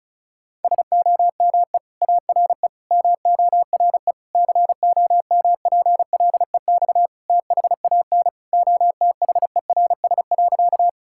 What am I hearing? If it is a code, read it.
Morse code, heard as a keyed tone, about 36 wpm, 700 Hz. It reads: SOME ARE MORE COMPLEX THAN OTHERS.
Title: Sentences from Top 1000 Words - Review Part 6 - 35wpm